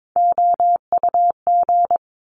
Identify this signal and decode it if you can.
Morse code, heard as a keyed tone, about 22 wpm, 700 Hz. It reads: OUG
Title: Top Three Letter Combinations in Words 22wpm